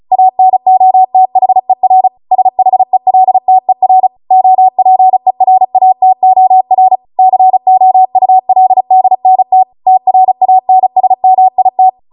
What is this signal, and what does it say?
Morse code, heard as a keyed tone, about 36 wpm, 750 Hz. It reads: ANOTHER SHELTER OPERATOR COULDNT TRANSMIT
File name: Morse / 35WPM